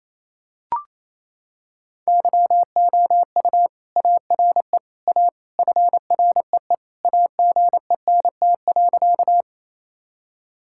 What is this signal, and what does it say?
Morse code, heard as a keyed tone, about 28 wpm, 700 Hz. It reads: YOU ARE A FREE AGENT.
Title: Sentences from Top 1600 Words 28wpm